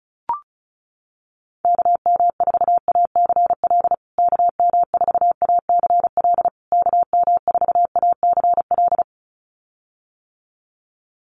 Morse code, heard as a keyed tone, about 34 wpm, 700 Hz. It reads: KM4ACL KM4ACL KM4ACL
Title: USA Call Signs Repeated 3x 35wpm